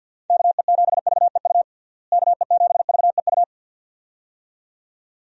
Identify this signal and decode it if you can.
Morse code, heard as a keyed tone, about 51 wpm, 700 Hz. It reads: KE7VEU KE7VEU
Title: USA Call Signs - 6 Characters - Repeated 2x - 50wpm